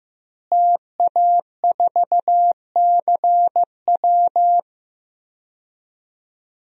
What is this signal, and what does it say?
Morse code, heard as a keyed tone, about 15 wpm, 700 Hz. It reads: TA4CW
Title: DXCC-100 Call Signs - 5 Characters - Repeated 2x - 15wpm